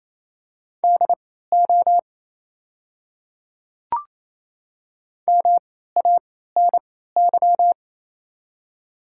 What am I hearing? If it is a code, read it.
Morse code, heard as a keyed tone, about 27 wpm, spaced as for 16 wpm, 700 Hz. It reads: DO MANY
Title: 3x Character Spacing - Top 300 Words - 28wpm